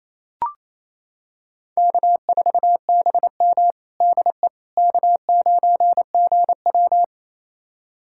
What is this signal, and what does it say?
Morse code, heard as a keyed tone, about 28 wpm, 700 Hz. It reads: K4BM DE K9GW
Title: USA Callsign DE Callsign - 4 Characters - 28wpm